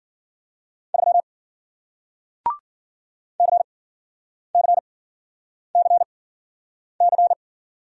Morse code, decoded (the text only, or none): 3 CCCC